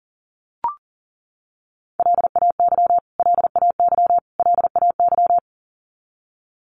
Morse code, heard as a keyed tone, about 40 wpm, 700 Hz. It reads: LAY LAY LAY
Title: Top 500 Words Repeated 40wpm